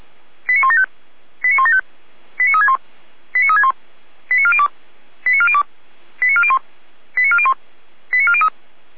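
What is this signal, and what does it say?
Five-tone selective call (ZVEI), heard as quick runs of five tones, tones of about 70 ms. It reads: 89175 89175 89361 89361 89402 89402 89401 89401 89403
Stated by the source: ZVEI five-tone call 89192 received by SDR.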